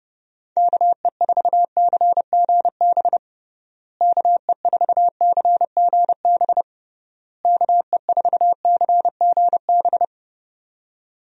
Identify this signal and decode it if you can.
Morse code, heard as a keyed tone, about 30 wpm, 700 Hz. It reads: KE4CGB KE4CGB KE4CGB
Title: USA Call Signs - 6 Characters - Repeated 3x - 30wpm